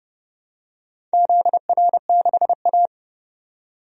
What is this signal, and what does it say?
Morse code, heard as a keyed tone, about 31 wpm, 700 Hz. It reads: ZR6A
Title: DXCC-100 Call Signs - 4 Characters - 30wpm